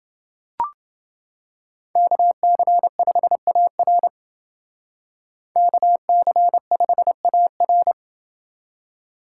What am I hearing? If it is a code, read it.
Morse code, heard as a keyed tone, about 29 wpm, 700 Hz. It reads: KC5AR KC5AR